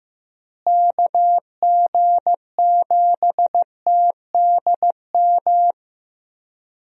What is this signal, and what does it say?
Morse code, heard as a keyed tone, about 15 wpm, 700 Hz. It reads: KG7TDM
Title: USA Call Signs - 6 Characters - Repeated 2x - 15wpm